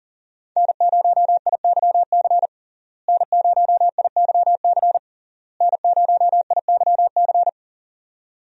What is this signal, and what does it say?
Morse code, heard as a keyed tone, about 40 wpm, 700 Hz. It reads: N0IYC N0IYC N0IYC
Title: USA Call Signs - 5 Characters - Repeated 3x - 40wpm